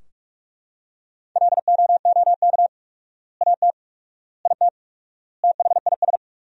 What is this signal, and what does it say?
Morse code, heard as a keyed tone, about 45 wpm, 700 Hz. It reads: LOOK AT IT THIS
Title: 4x Word-Spacing - Sentences from Top 100 Words - 45wpm